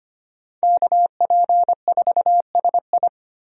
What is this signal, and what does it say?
Morse code, heard as a keyed tone, about 25 wpm, 700 Hz. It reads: KP4SI
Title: USA Call Signs - 5 Characters - 25wpm